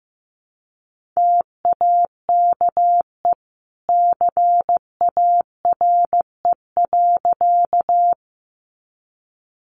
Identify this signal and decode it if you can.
Morse code, heard as a keyed tone, about 15 wpm, 700 Hz. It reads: TAKE CARE.
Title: Sentences from Top 300 Words 15wpm